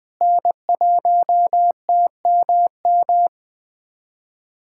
Morse code, heard as a keyed tone, about 20 wpm, 700 Hz. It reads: N1TMM
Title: USA Callsign DE Callsign - 5 Characters - 20wpm